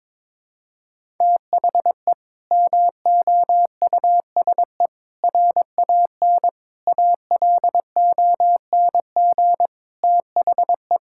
Morse code, heard as a keyed tone, about 22 wpm, 700 Hz. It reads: THE MOUSE RAN ALONG THE
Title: Sentences from Top 1000 Words - Review Part 5 - 22wpm